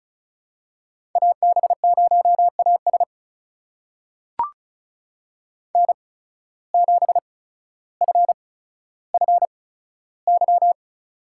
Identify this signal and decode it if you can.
Morse code, heard as a keyed tone, about 36 wpm, spaced as for 15 wpm, 700 Hz. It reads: AB0AS N7FFY
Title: USA Call Signs - 5 Characters - 8x Character Spacing - 35wpm